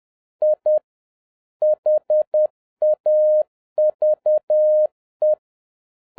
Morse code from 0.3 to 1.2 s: I